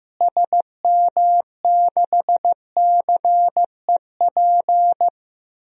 Morse code, heard as a keyed tone, about 15 wpm, 700 Hz. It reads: SM6CEP
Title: DXCC-100 Call Signs - 6 Characters - Repeated 3x - 15wpm